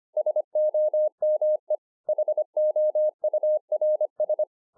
Morse code, heard as a keyed tone, about 25 wpm, 610 Hz. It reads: SOME HOURS